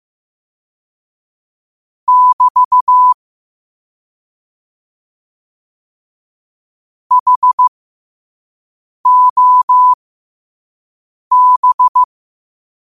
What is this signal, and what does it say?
Morse code, heard as a keyed tone, about 15 wpm, spaced as for 5 wpm, 1000 Hz. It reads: <BT> HOB
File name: 5wpm